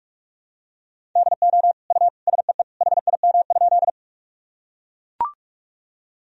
Morse code, heard as a keyed tone, about 42 wpm, 700 Hz. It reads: DO U SEE HIM?
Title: Sentences from Top 100 Words and Abbreviations 45wpm